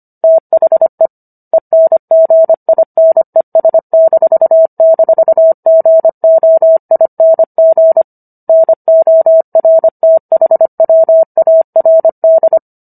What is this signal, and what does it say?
Morse code, heard as a keyed tone, about 25 wpm, 650 Hz. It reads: THE ENGINES--GOING NORTHWARD